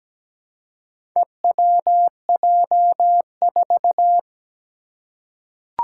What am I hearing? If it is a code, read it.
Morse code, heard as a keyed tone, about 17 wpm, 700 Hz. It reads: EWJ4
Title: Four Challenging Letter-Number Combinations 17wpm